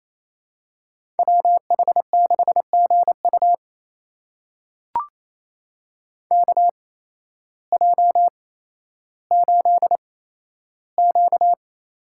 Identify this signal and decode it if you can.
Morse code, heard as a keyed tone, about 28 wpm, spaced as for 13 wpm, 700 Hz. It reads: WH6GU KJ8Q
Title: USA Call Signs - 8x Character Spacing - 28wpm